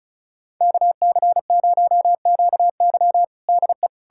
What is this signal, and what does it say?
Morse code, heard as a keyed tone, about 36 wpm, 700 Hz. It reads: KC0QY DE
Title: USA Callsign DE Callsign - 5 Characters - 35wpm